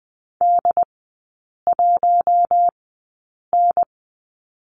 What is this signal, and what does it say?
Morse code, heard as a keyed tone, about 20 wpm, spaced as for 8 wpm, 700 Hz. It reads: D1N
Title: 067 Lesson 12e - Letter d - Three Characters 20wpm